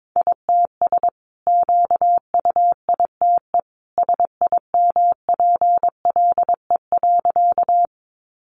Morse code, heard as a keyed tone, about 22 wpm, 700 Hz. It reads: ITS QUITE SIMPLE.
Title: Sentences from Top 700 Words 22wpm